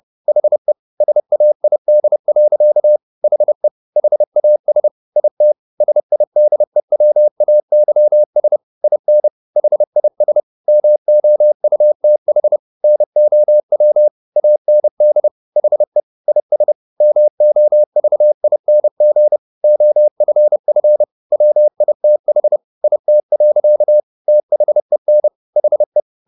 Morse code, heard as a keyed tone, about 30 wpm, 600 Hz. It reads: HE SAID. HE HAS IT SIDEWAYS IN HIS MOUTH NOW AND HE IS MOVING OFF WITH IT. THEN HE
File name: the old man and the sea / the old man and the sea 1 30wpm